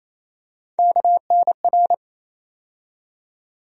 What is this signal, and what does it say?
Morse code, heard as a keyed tone, about 28 wpm, 700 Hz. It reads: KNR
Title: USA Call Sign Postfixes - 3 Letters 28wpm